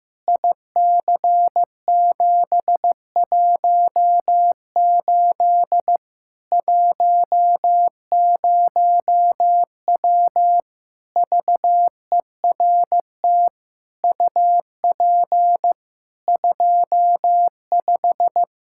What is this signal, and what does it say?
Morse code, heard as a keyed tone, about 15 wpm, 700 Hz. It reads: IC718 10W VERT UP 25
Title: Practice QSO - Exchange 2 - Info - 15wpm